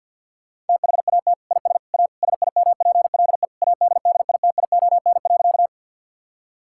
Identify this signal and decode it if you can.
Morse code, heard as a keyed tone, about 51 wpm, 700 Hz. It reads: THAT IS A SIMPLE ADDITION.
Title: Sentences from Top 1000 Words - Review Part 6 - 50wpm